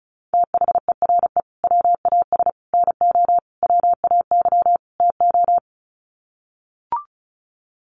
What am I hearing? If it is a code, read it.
Morse code, heard as a keyed tone, about 34 wpm, 700 Hz. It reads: THERE WAS NO WAY TO
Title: Sets of 5 Words Encore 35wpm